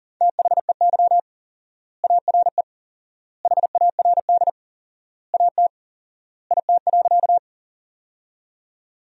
Morse code, heard as a keyed tone, about 40 wpm, 700 Hz. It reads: THEY ARE HARD AT IT.
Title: 4x Word-Spacing - Sentences from Top 300 Words - 40wpm